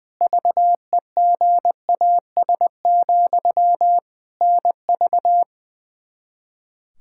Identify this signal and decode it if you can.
Morse code, heard as a keyed tone, about 20 wpm, 700 Hz. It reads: VEGAS, NV
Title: K1USN SST QSO Practice 20wpm